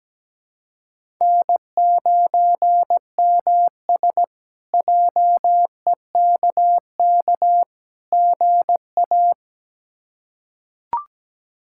Morse code, heard as a keyed tone, about 17 wpm, 700 Hz. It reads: N9MS JEKK GA
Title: K1USN SST QSO Practice - 17wpm